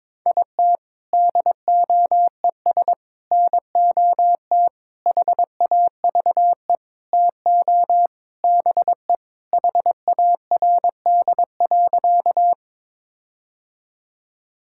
Morse code, heard as a keyed tone, about 21 wpm, 700 Hz. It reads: IT DOES NOT HAVE TO BE HARD.
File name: practice / 22wpm / Sentences with 4-Word Phrases - 22wpm